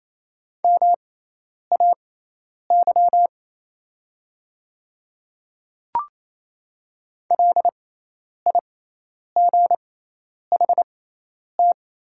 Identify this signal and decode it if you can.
Morse code, heard as a keyed tone, about 28 wpm, spaced as for 10 wpm, 700 Hz. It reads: MAY LIGHT
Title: 6x Character Spacing - Top 500 Words - 28wpm